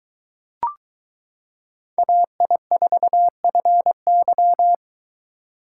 Morse code, heard as a keyed tone, about 22 wpm, 700 Hz. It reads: AI4FY